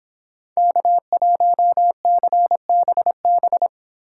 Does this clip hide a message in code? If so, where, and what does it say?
Yes: Morse code from 0.6 to 3.7 s, heard as a keyed tone, about 26 wpm, 700 Hz. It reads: K1CBB